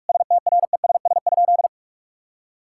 Morse code, heard as a keyed tone, about 44 wpm, 700 Hz. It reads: STRESS?
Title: Sentences from Top 1000 Words - Review Part 6 - 45wpm